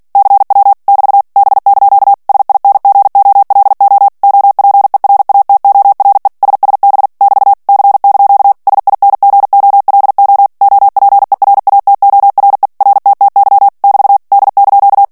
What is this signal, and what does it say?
Morse code, heard as a keyed tone, about 48 wpm, 800 Hz. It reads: CW <BT> B) SINGOLO OPERATORE SSB <BT> C) SINGOLO OPERATORE RTTY <BT> D)
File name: ITALIEN texte clair 2 vitesse 45 mots minute